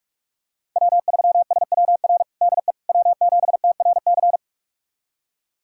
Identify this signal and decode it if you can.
Morse code, heard as a keyed tone, about 46 wpm, 700 Hz. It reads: W3SWR DE W7TRC
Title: USA Callsign DE Callsign - 5 Characters - 45wpm